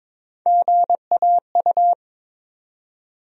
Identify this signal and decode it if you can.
Morse code, heard as a keyed tone, about 22 wpm, 700 Hz. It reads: GAU
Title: USA Call Sign Postfixes - 3 Letters 22wpm